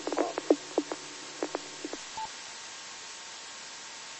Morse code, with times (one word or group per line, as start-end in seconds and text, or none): none